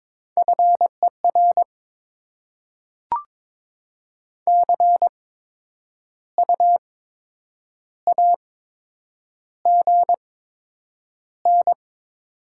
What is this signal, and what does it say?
Morse code, heard as a keyed tone, about 22 wpm, spaced as for 8 wpm, 700 Hz. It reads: FER CU AGN
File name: Top 20 QSO Elements - 8x Character Spacing - 22wpm